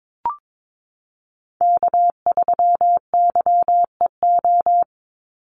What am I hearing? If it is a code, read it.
Morse code, heard as a keyed tone, about 22 wpm, 700 Hz. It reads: K3YEO
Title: Call Signs - 22wpm